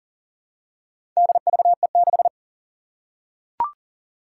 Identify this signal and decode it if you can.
Morse code, heard as a keyed tone, about 40 wpm, 700 Hz. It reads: DVE6
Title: Four Challenging Letter-Number Combinations 40wpm